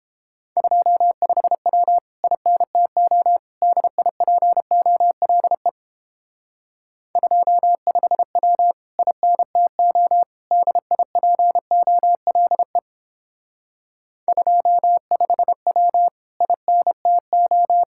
Morse code, heard as a keyed tone, about 29 wpm, 700 Hz. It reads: 25W INTO DIPOLE 25W INTO DIPOLE 25W INTO